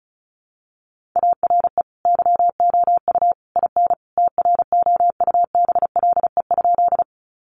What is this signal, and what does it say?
Morse code, heard as a keyed tone, about 34 wpm, 700 Hz. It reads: ARE YOU IN TROUBLE?
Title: Sentences from Top 1200 Words 35wpm